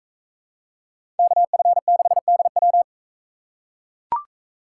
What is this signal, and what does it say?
Morse code, heard as a keyed tone, about 42 wpm, 700 Hz. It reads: KF6DW